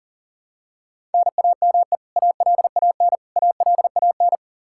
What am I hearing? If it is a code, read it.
Morse code, heard as a keyed tone, about 37 wpm, 700 Hz. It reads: NAME ALAN ALAN
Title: Practice QSO - Exchange 1 - Beginning - 40wpm